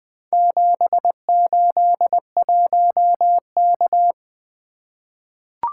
Morse code, heard as a keyed tone, about 20 wpm, 700 Hz. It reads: 781K
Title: Four Letter-Number Combinations - 20wpm